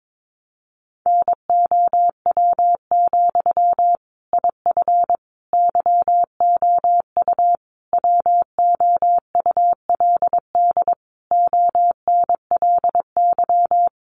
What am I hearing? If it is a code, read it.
Morse code, heard as a keyed tone, about 22 wpm, 700 Hz. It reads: NOW, IF YOU WOULD ONLY
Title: Sentences from Top 900 Words 22wpm